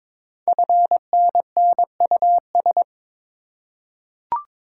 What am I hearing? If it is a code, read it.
Morse code, heard as a keyed tone, about 22 wpm, 700 Hz. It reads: FNNUS
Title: Code Groups - 5 Letters - 22wpm